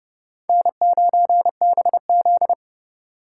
Morse code, heard as a keyed tone, about 28 wpm, 700 Hz. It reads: N9BZ